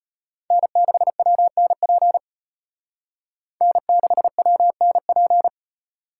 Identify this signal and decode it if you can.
Morse code, heard as a keyed tone, about 33 wpm, 700 Hz. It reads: N6WNP N6WNP